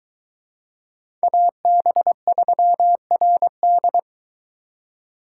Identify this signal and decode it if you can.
Morse code, heard as a keyed tone, about 24 wpm, 700 Hz. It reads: AB3RD